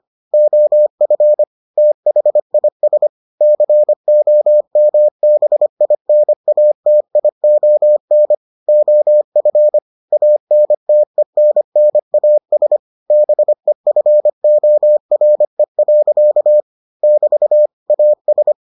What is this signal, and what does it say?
Morse code, heard as a keyed tone, about 25 wpm, 600 Hz. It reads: OF THIS COMBINATION OF ANTENNAS BEFORE. <BT> AS